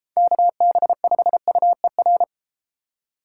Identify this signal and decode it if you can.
Morse code, heard as a keyed tone, about 31 wpm, 700 Hz. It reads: KB5UER